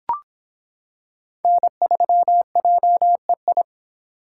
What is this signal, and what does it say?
Morse code, heard as a keyed tone, about 25 wpm, 700 Hz. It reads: N3JEI